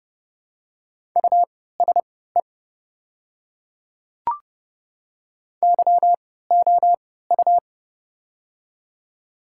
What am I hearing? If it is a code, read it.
Morse code, heard as a keyed tone, about 30 wpm, spaced as for 17 wpm, 690 Hz. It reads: USE YOU
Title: 3x Character Spacing - Top 100 Words - 30wpm